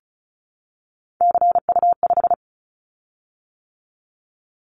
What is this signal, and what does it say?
Morse code, heard as a keyed tone, about 36 wpm, 700 Hz. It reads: CU5
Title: Callsign Prefixes for Top 50 DXCC - 35wpm